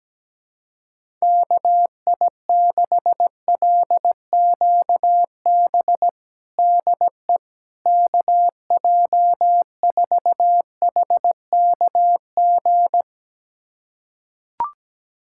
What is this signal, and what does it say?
Morse code, heard as a keyed tone, about 17 wpm, 700 Hz. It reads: KI6LQB DE KJ4HKG
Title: USA Callsign DE Callsign - 6 Characters - 17wpm